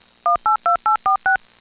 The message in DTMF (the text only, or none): 182846